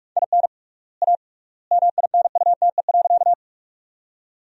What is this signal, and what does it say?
Morse code, heard as a keyed tone, about 42 wpm, 700 Hz. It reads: IN A MINUTE.
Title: 3x Word-Spacing - Sentences from Top 500 Words - 45wpm